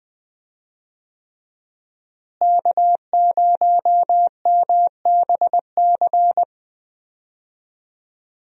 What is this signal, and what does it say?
Morse code, heard as a keyed tone, about 21 wpm, 700 Hz. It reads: K0MBC
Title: USA Call Signs - 5 Characters - Repeated 3x - 20wpm